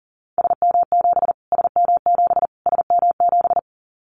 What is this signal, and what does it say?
Morse code, heard as a keyed tone, about 40 wpm, 700 Hz. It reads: SM7 SM7 SM7
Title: Callsign Prefixes for Top 25 DXCC - Repeated 3x - 40wpm